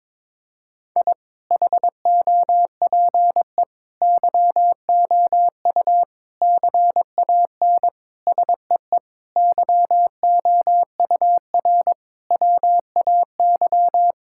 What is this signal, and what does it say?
Morse code, heard as a keyed tone, about 22 wpm, 700 Hz. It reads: I HOPE YOU CAN SEE YOUR WAY
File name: practice / 22wpm / Sets of 8 Words - 22wpm